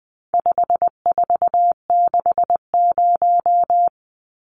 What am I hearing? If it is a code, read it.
Morse code, heard as a keyed tone, about 20 wpm, 700 Hz. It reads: 5460
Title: Numbers - 4-Digit 20wpm